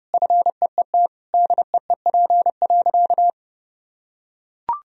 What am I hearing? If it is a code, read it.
Morse code, heard as a keyed tone, about 30 wpm, 700 Hz. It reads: FEET DEEP.
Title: Sentences from Top 1000 Words - Review Part 3 - 30wpm